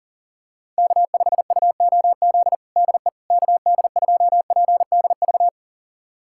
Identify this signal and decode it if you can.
Morse code, heard as a keyed tone, about 40 wpm, 700 Hz. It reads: K5UOZ DE KD2PDV
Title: USA Callsign DE Callsign 40wpm